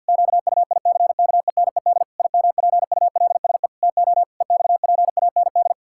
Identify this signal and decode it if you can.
Morse code, heard as a keyed tone, about 50 wpm, 700 Hz. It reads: QUICKENED IMPULSE TO EXPAND
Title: Pangram Challenge v2 - 50wpm